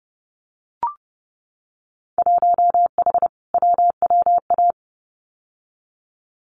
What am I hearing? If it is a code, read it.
Morse code, heard as a keyed tone, about 31 wpm, 700 Hz. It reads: 1H WWA
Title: Winter Field Day Exchanges 30wpm